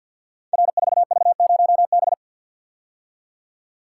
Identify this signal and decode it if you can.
Morse code, heard as a keyed tone, about 51 wpm, 700 Hz. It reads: R4V0B
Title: Code Groups - 5 Letter:Numbers - 50wpm